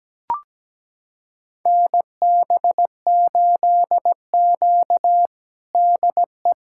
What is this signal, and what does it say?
Morse code, heard as a keyed tone, about 17 wpm, 700 Hz. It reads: NB8Q DE
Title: USA Callsign DE Callsign - 4 Characters - 17wpm